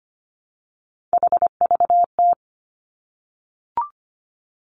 Morse code, heard as a keyed tone, about 25 wpm, 700 Hz. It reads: HVT
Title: Three Challenging Letter-Number Combinations 25wpm